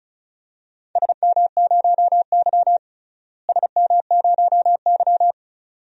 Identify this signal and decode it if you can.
Morse code, heard as a keyed tone, about 36 wpm, 700 Hz. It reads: SM0Y SM0Y
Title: DXCC-100 Call Signs - 4 Characters - Repeated 2x - 35wpm